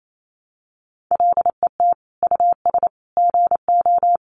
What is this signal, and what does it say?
Morse code, heard as a keyed tone, about 28 wpm, 700 Hz. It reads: LET US GO
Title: Sentences from Top 1100 Words 28wpm